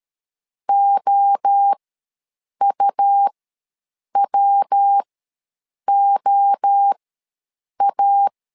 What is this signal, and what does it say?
Morse code, heard as a keyed tone, about 13 wpm, spaced as for 7 wpm, 800 Hz. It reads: OUWOA
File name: LEZIONI / 3 MWRO / 3 MWRO ws 12wpm 1914